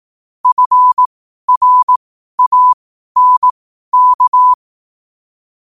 Morse code, heard as a keyed tone, about 18 wpm, spaced as for 12 wpm, 1000 Hz. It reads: FRANK
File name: 12wpm